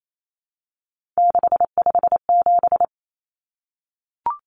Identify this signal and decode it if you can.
Morse code, heard as a keyed tone, about 28 wpm, 700 Hz. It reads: 657
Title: Common Cut Numbers - 3-Digit 28wpm